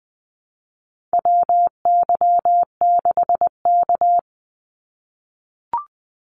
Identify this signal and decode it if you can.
Morse code, heard as a keyed tone, about 20 wpm, 700 Hz. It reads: WY6K
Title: Call Signs - 20wpm